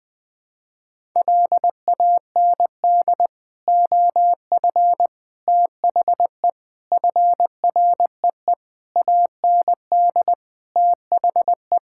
Morse code, heard as a keyed tone, about 21 wpm, 700 Hz. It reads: LAND OF THE FREE AND THE
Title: Sentences with 9-Word Phrases - Spoken Beginning - 20wpm